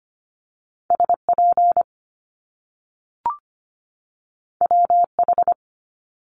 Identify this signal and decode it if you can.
Morse code, heard as a keyed tone, about 25 wpm, 700 Hz. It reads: SP WH
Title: Top Two Letter Combinations in Words 25wpm